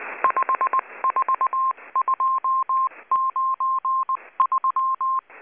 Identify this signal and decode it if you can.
Morse code, heard as a keyed tone, about 19 wpm, 1060 Hz. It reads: 54293
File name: HM02- -0540utc